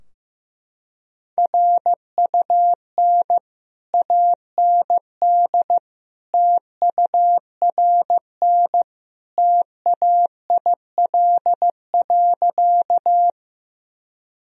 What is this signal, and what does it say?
Morse code, heard as a keyed tone, about 15 wpm, 700 Hz. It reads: RUN AND TURN TAIL.
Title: Sentences from Top 1000 Words - Review Part 3 - 15wpm